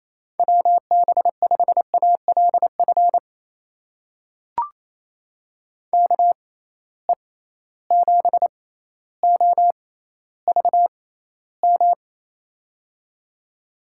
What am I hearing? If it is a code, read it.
Morse code, heard as a keyed tone, about 28 wpm, spaced as for 14 wpm, 700 Hz. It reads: WB5ALF KE7OVM
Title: USA Call Signs - 6 Characters - 6x Character Spacing - 28wpm